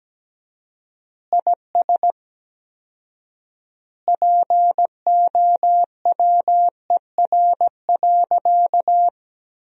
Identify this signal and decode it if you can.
Morse code, heard as a keyed tone, about 17 wpm, 700 Hz. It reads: IS POWER.
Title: 4x Word-Spacing - Sentences from Top 500 Words - 17wpm